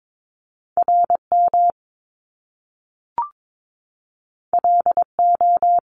Morse code, heard as a keyed tone, about 22 wpm, 700 Hz. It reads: RM LO
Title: Top Two Letter Combinations in Words 22wpm